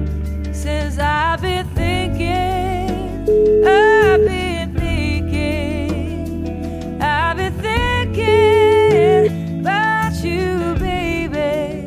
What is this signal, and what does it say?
Ringback tone (Europe, 425 Hz), with noise.